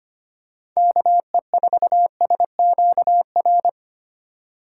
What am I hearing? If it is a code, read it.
Morse code, heard as a keyed tone, about 25 wpm, 700 Hz. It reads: KE4SQR
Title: USA Call Signs - 6 Characters - 3x Character Spacing - 25wpm